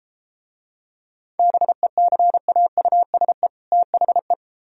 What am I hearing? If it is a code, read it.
Morse code, heard as a keyed tone, about 32 wpm, 700 Hz. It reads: BECAUSE THE